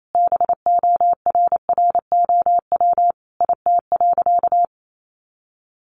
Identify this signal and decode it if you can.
Morse code, heard as a keyed tone, about 28 wpm, 700 Hz. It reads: BORROW IT.
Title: Sentences from Top 1500 Words 28wpm